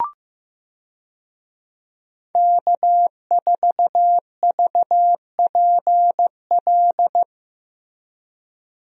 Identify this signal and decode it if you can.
Morse code, heard as a keyed tone, about 15 wpm, 700 Hz. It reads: K4VPL
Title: USA Call Signs - 5 Characters - Repeated 3x - 15wpm